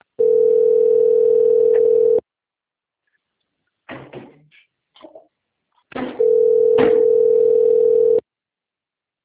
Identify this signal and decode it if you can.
Ringback tone (North America), 440 + 480 Hz.